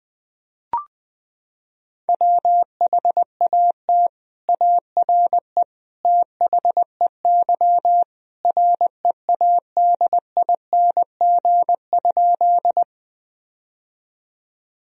Morse code, heard as a keyed tone, about 20 wpm, 700 Hz. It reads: WHAT ARE THEY READING?
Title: Sentences from Top 1000 Words - Review Part 3 - 20wpm